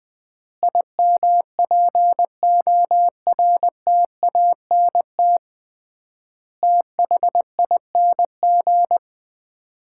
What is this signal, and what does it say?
Morse code, heard as a keyed tone, about 20 wpm, 700 Hz. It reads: IMPORTANT THING
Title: 3x Word-Spacing - Sets of 4 Words - 20wpm